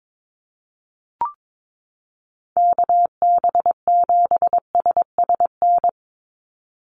Morse code, heard as a keyed tone, about 22 wpm, 700 Hz. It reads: KB7SSN
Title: USA Call Signs 22wpm